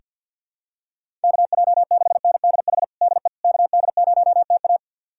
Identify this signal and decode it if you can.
Morse code, heard as a keyed tone, about 45 wpm, 700 Hz. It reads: KJ6NDH DE KD0TA
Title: USA Callsign DE Callsign 50wpm